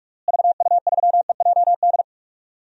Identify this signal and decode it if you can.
Morse code, heard as a keyed tone, about 41 wpm, 700 Hz. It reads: VU3EJD
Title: DXCC-100 Call Signs - 6 Characters - Repeated 3x - 45wpm